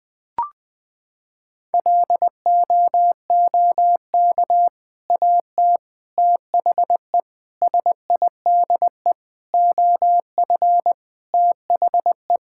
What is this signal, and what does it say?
Morse code, heard as a keyed tone, about 20 wpm, 700 Hz. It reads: LOOK AT THE SIDE OF THE
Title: Sentences with 3-Word Phrases - 20wpm